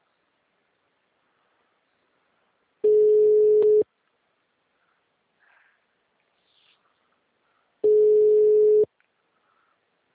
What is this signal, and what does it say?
Ringback tone (Europe, 425 Hz).
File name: out- -2001- - - .53673